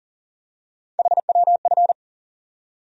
Morse code, heard as a keyed tone, about 40 wpm, 700 Hz. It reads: HWF
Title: USA Call Sign Postfixes - 3 Letters 40wpm